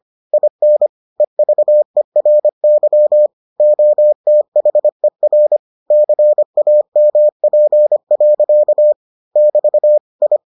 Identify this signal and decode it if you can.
Morse code, heard as a keyed tone, about 25 wpm, 600 Hz. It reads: IN EVERY OTHER CAMP. <BT> I